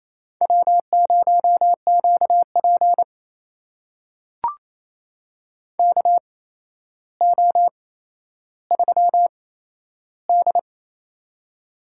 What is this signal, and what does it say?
Morse code, heard as a keyed tone, about 29 wpm, spaced as for 12 wpm, 700 Hz. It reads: W0QP KO3D
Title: USA Call Signs - 4 Characters - 8x Character Spacing - 28wpm